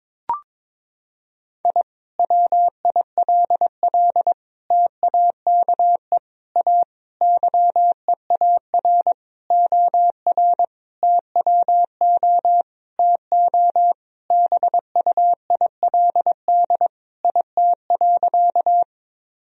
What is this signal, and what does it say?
Morse code, heard as a keyed tone, about 23 wpm, 700 Hz. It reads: I WILL TAKE A YEAR OR TWO TO BUILD IT.
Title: Sentences from Top 1000 Words - Review Part 1 - 22wpm